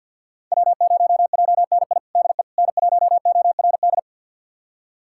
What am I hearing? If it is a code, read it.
Morse code, heard as a keyed tone, about 51 wpm, 700 Hz. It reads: W0JNI DE N1ORD
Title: USA Callsign DE Callsign - 5 Characters - 50wpm